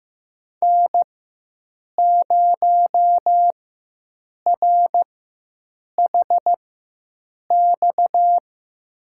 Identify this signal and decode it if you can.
Morse code, heard as a keyed tone, about 15 wpm, spaced as for 7 wpm, 700 Hz. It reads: N0RHX
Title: USA Call Signs - 5 Characters - 4x Character Spacing - 15wpm